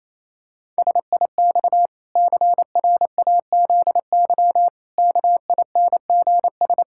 Morse code, heard as a keyed tone, about 28 wpm, 700 Hz. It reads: SIX CRAZY KINGS
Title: Pangram Challenge v2 - 28wpm